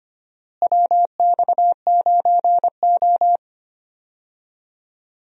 Morse code, heard as a keyed tone, about 25 wpm, 700 Hz. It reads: WX9O
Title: USA Call Signs - 4 Characters - Repeated 2x - 25wpm